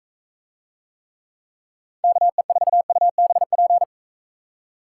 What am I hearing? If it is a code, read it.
Morse code, heard as a keyed tone, about 45 wpm, 700 Hz. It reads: KE4UBP